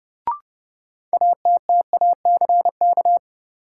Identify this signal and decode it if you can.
Morse code, heard as a keyed tone, about 30 wpm, 700 Hz. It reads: ATTACK
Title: Sentences from Top 1500 Words - New Word Drill Alt - 30wpm